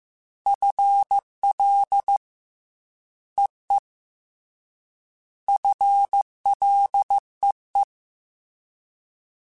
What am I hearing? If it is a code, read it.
Morse code, heard as a keyed tone, about 15 wpm, 800 Hz. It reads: FL EE FLEE
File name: FL / FL mp3s / Fast FL @13